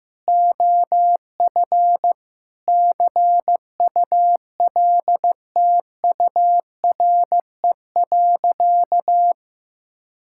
Morse code, heard as a keyed tone, about 15 wpm, 700 Hz. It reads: OF CULTURE.
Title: Sentences from Top 1000 Words - Review Part 6 - 15wpm